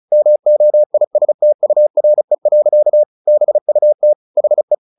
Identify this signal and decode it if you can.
Morse code, heard as a keyed tone, about 34 wpm, 600 Hz. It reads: MOISTURE. BUT HE
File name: the old man and the sea 2 35wpm